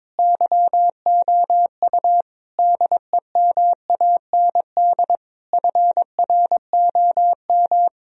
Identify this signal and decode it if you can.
Morse code, heard as a keyed tone, about 21 wpm, 700 Hz. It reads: YOU DEMAND FROM
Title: Sentences from Top 1000 Words - Review Part 4 - 22wpm